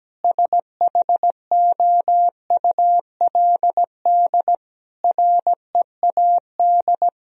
Morse code, heard as a keyed tone, about 17 wpm, 700 Hz. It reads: SHOULD READ
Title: Sentences from Top 1000 Words - Review Part 4 - 17wpm